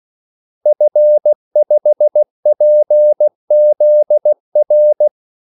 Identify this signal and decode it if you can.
Morse code, heard as a keyed tone, about 16 wpm, 600 Hz. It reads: F5PZR
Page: Kaldesignaler 16 wpm